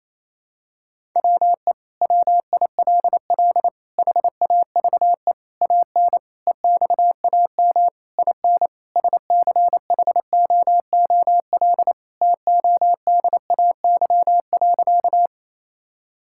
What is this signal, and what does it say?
Morse code, heard as a keyed tone, about 27 wpm, 700 Hz. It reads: WE WILL HAVE AN EXAM IN SCHOOL TODAY.
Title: Sentences from Top 1000 Words - Review Part 5 - 28wpm